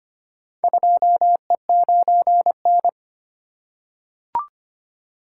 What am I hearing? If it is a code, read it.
Morse code, heard as a keyed tone, about 25 wpm, 700 Hz. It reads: 2E9N
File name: practice / 25wpm / Four Letter-Number Combinations - 25wpm